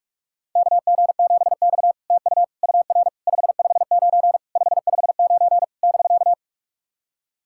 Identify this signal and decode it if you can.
Morse code, heard as a keyed tone, about 46 wpm, 700 Hz. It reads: KG7X TU UR 559 559 <BK>
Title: SOTA QSO Practice 45wpm